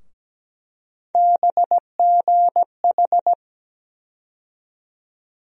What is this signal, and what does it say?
Morse code, heard as a keyed tone, about 17 wpm, 700 Hz. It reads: BGH
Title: USA Call Sign Postfixes - 3 Letters 17wpm